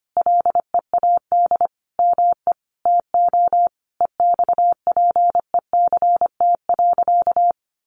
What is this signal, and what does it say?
Morse code, heard as a keyed tone, about 25 wpm, 700 Hz. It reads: LEAD ME TO EXPECT.
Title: Sentences from Top 1200 Words 25wpm